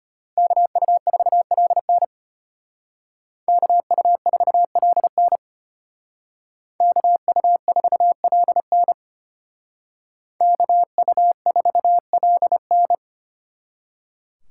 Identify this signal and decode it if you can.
Morse code, heard as a keyed tone, about 31 wpm, 700 Hz. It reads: KU4LN KU4LN KU4LN KU4LN
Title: Speed Racing - USA 2x2 Call Signs 25wpm